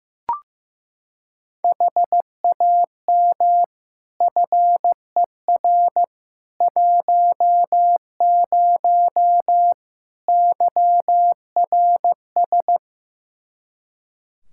Morse code, heard as a keyed tone, about 15 wpm, 700 Hz. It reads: HAM FER 10 YRS
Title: K1USN SST QSO Practice 15wpm